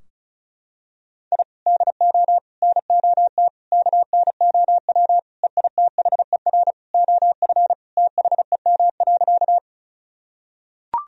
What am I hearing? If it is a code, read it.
Morse code, heard as a keyed tone, about 34 wpm, 700 Hz. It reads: I DO NOT KNOW EITHER OF THEM.
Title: Sentences from Top 1000 Words - Review Part 4 - 35wpm